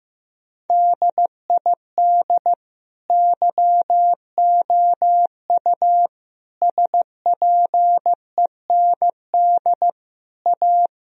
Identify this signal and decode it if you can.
Morse code, heard as a keyed tone, about 15 wpm, 700 Hz. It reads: DID YOU SPEND A
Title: Sentences from Top 1000 Words - Review Part 4 - 15wpm